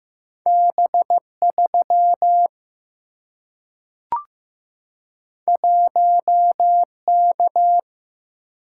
Morse code, heard as a keyed tone, about 15 wpm, 700 Hz. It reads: B3 1K
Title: Single-Letter Single-Number 15wpm